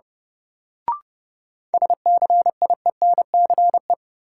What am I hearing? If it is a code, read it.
Morse code, heard as a keyed tone, about 30 wpm, 700 Hz. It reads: SCIENCE
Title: Sentences from Top 1000 Words - Review Part 2 - New Word Drill - 30wpm